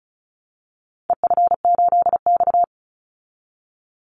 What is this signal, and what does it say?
Morse code, heard as a keyed tone, about 34 wpm, 700 Hz. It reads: EF8X
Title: Call Signs Repeated 2x - 35wpm